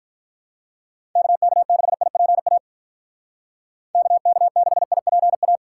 Morse code, heard as a keyed tone, about 46 wpm, 700 Hz. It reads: KK6IPA KK6IPA